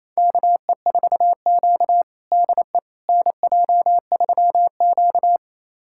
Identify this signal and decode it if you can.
Morse code, heard as a keyed tone, about 28 wpm, 700 Hz. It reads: KE4Q DE NJ3Q
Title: USA Callsign DE Callsign - 4 Characters - 28wpm